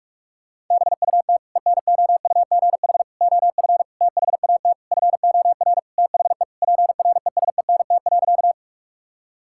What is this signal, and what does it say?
Morse code, heard as a keyed tone, about 46 wpm, 700 Hz. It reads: BUT ENOUGH OF THAT FOR THE PRESENT.
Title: Sentences from Top 1000 Words - Review Part 5 - 45wpm